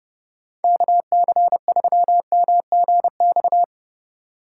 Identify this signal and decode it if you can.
Morse code, heard as a keyed tone, about 30 wpm, 700 Hz. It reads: KC3MGX
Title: USA Call Signs - 4x Character Spacing - 30wpm